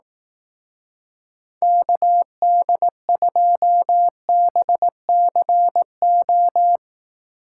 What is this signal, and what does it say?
Morse code, heard as a keyed tone, about 18 wpm, 700 Hz. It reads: KD2BCO
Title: Speed Racing - USA 2x3 Call Signs 15wpm